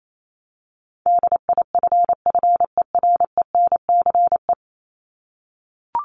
Morse code, heard as a keyed tone, about 28 wpm, 700 Hz. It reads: DIFFERENCE
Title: Top 751-1000 Words 28wpm